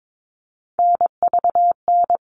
Morse code, heard as a keyed tone, about 22 wpm, 700 Hz. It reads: NVN
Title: All Cut Numbers - 3-Digit 22wpm